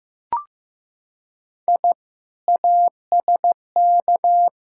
Morse code, heard as a keyed tone, about 15 wpm, 700 Hz. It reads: I ASK
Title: Sentences from Top 1000 Words - Review Part 5 - 15wpm